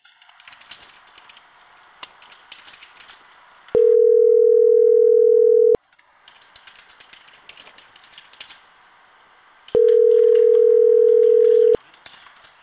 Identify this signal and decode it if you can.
Ringback tone (North America), 440 + 480 Hz.